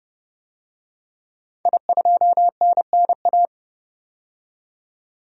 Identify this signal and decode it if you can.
Morse code, heard as a keyed tone, about 30 wpm, 700 Hz. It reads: I2NNA
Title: DXCC-100 Call Signs - 5 Characters - 30wpm